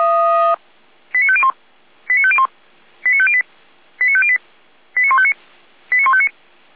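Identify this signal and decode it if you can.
Five-tone selective call (ZVEI), heard as quick runs of five tones, tones of about 70 ms. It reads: 89501 89501 89508 89508 89159 89159
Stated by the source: ZVEI five-tone selective call received with an SDR.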